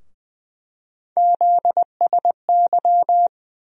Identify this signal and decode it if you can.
Morse code, heard as a keyed tone, about 20 wpm, 700 Hz. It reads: ZSY
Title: USA Call Sign Postfixes - 3 Letters 20wpm